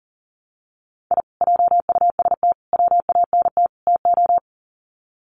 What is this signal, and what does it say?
Morse code, heard as a keyed tone, about 40 wpm, 700 Hz. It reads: I JUST WANT TO
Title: Sets of 4 Words 40wpm